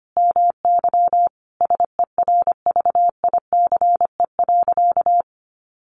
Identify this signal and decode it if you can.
Morse code, heard as a keyed tone, about 25 wpm, 700 Hz. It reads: MY SERVICE.
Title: Sentences from Top 700 Words 25wpm